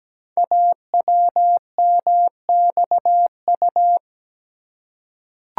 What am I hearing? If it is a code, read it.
Morse code, heard as a keyed tone, about 17 wpm, 700 Hz. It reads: AWMXU
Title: Code Groups - 5 Letters - 17wpm